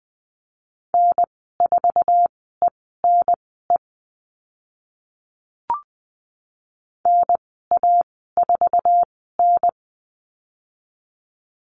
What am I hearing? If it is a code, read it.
Morse code, heard as a keyed tone, about 20 wpm, spaced as for 14 wpm, 700 Hz. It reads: N4ENE NA4N
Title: 076 Lesson 13g - Letter L - Call Signs 20wpm